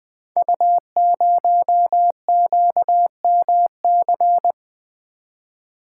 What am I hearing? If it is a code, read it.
Morse code, heard as a keyed tone, about 20 wpm, 700 Hz. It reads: U0QMC
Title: Code Groups - 5 Letter:Numbers - 20wpm